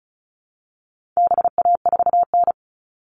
Morse code, heard as a keyed tone, about 34 wpm, 700 Hz. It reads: BA4N
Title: All Cut Numbers - 4-Digit 35wpm